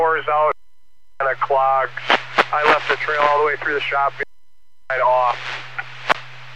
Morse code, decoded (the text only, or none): none